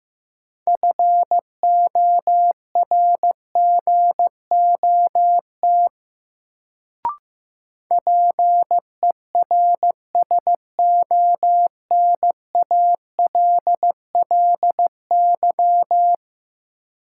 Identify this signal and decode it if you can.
Morse code, heard as a keyed tone, about 15 wpm, 700 Hz. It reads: FORGOT PERSONALLY
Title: Sentences from Top 1400 Words - New Word Drill - 15wpm